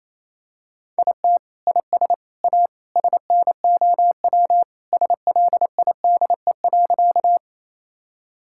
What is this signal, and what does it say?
Morse code, heard as a keyed tone, about 28 wpm, 700 Hz. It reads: IT IS A SNOW SLIDE.
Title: Sentences from Top 1600 Words 28wpm